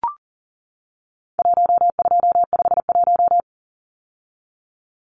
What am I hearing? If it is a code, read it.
Morse code, heard as a keyed tone, about 40 wpm, 700 Hz. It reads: 1251
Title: Numbers - Contest Style - 4-Digit 40wpm